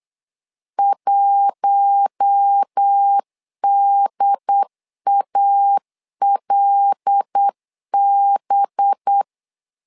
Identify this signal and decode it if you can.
Morse code, heard as a keyed tone, about 8 wpm, 800 Hz. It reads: 1DALB
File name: LEZIONI / 5 LBVC / 5 LBVC 8wpm 2935